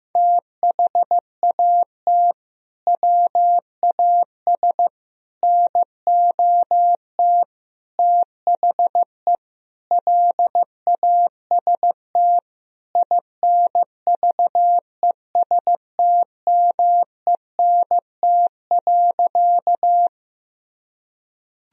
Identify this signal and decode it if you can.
Morse code, heard as a keyed tone, about 15 wpm, 700 Hz. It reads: THAT WAS NOT THE LAST INVESTMENT.
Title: Sentences from Top 1000 Words - Review Part 4 - 15wpm